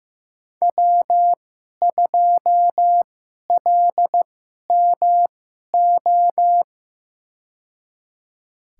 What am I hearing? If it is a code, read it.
Morse code, heard as a keyed tone, about 15 wpm, spaced as for 11 wpm, 700 Hz. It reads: W2LMO
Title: USA Call Signs - 5 Characters - 2x Character Spacing - 15wpm